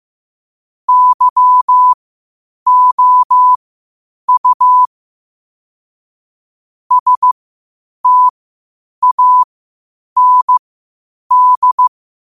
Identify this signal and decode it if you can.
Morse code, heard as a keyed tone, about 15 wpm, spaced as for 8 wpm, 1000 Hz. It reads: YOU STAND
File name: qotdfiles / 8wpm